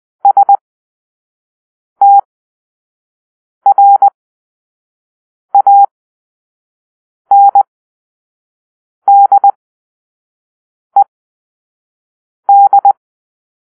Morse code, heard as a keyed tone, about 20 wpm, spaced as for 5 wpm, 800 Hz. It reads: STRANDED
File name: arrl / Grp-2 Drill-3